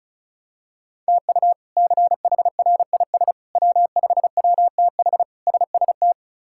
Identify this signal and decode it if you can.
Morse code, heard as a keyed tone, about 36 wpm, 700 Hz. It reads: TU CHRIS W5WTH SST
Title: K1USN SST QSO Practice - 35wpm